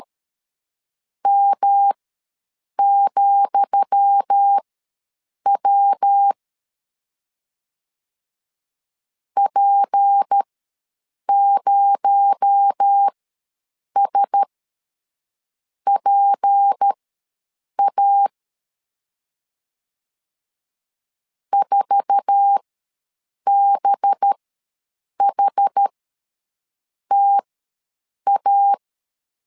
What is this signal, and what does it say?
Morse code, heard as a keyed tone, about 13 wpm, spaced as for 7 wpm, 800 Hz. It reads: M,W P0SPA 4BHTA
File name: LEZIONI / 10 SIGNS / 10D COMA ws 12wpm 3618